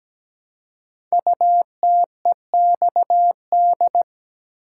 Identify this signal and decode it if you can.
Morse code, heard as a keyed tone, about 17 wpm, 700 Hz. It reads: UTEXD
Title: Code Groups - 5 Letters - 17wpm